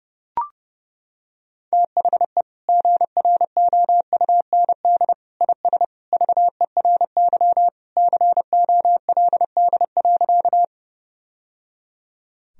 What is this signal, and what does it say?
Morse code, heard as a keyed tone, about 30 wpm, 700 Hz. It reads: THE GROUND IS VERY COLD.
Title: Sentences from Top 1000 Words - Review Part 2 - 30wpm